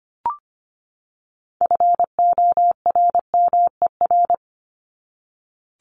Morse code, heard as a keyed tone, about 25 wpm, 700 Hz. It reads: FORMER
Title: LY Suffix - Top 100 Words with Base Words 25wpm